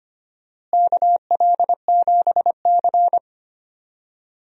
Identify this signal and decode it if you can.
Morse code, heard as a keyed tone, about 25 wpm, 700 Hz. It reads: KL7C
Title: DXCC-100 Call Signs - 4 Characters - Repeated 3x - 25wpm